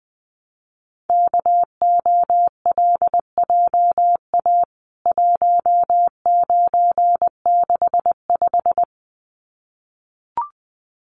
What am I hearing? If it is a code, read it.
Morse code, heard as a keyed tone, about 20 wpm, 700 Hz. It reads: KOLJA 1965
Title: CWT Exchange 20wpm